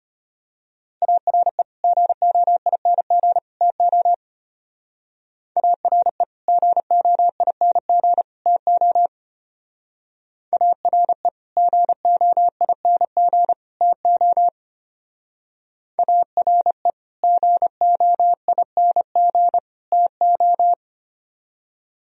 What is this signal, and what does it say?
Morse code, heard as a keyed tone, about 32 wpm, 700 Hz. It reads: ARE GOING TO ARE GOING TO ARE GOING TO ARE GOING TO